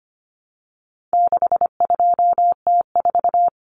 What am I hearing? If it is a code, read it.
Morse code, heard as a keyed tone, about 25 wpm, 700 Hz. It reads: 62T4
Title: Common Cut Numbers - 4-Digit 25wpm